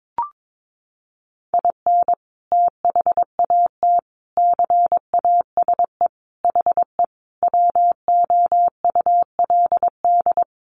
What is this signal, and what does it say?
Morse code, heard as a keyed tone, about 22 wpm, 700 Hz. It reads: IN THAT CASE HE WOULD
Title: Sentences from Top 800 Words 22wpm